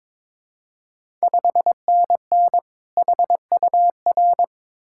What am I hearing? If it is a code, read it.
Morse code, heard as a keyed tone, about 22 wpm, 700 Hz. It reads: 5NN HUR
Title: Ontario QSO Party Exchange 22wpm